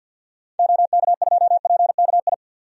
Morse code, heard as a keyed tone, about 53 wpm, 700 Hz. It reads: OK2PKI
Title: DXCC-100 Call Signs - 6 Characters - 50wpm